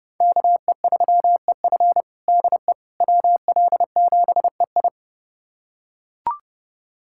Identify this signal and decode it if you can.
Morse code, heard as a keyed tone, about 30 wpm, 700 Hz. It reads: KE3EF DE WL7EI
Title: USA Callsign DE Callsign - 5 Characters - 30wpm